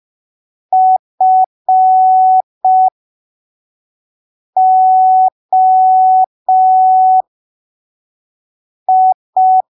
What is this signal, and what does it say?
Morse code, heard as a keyed tone, about 5 wpm, spaced as for 3 wpm, 750 Hz. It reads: FOI
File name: arquivos / texto / texto 17 05ppm